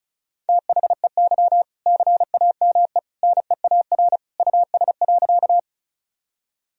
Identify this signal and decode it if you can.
Morse code, heard as a keyed tone, about 34 wpm, 700 Hz. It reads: THEY CAME NEAR US.
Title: Sentences from Top 1000 Words - Review Part 1 - 35wpm